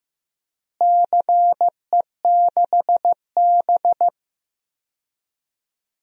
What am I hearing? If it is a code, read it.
Morse code, heard as a keyed tone, about 15 wpm, 700 Hz. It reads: CE6B
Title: DXCC-100 Call Signs - 4 Characters - Repeated 2x - 15wpm